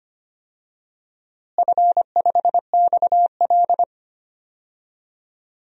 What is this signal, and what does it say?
Morse code, heard as a keyed tone, about 25 wpm, 700 Hz. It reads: F5XL
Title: DXCC-100 Call Signs - 4 Characters - Repeated 3x - 25wpm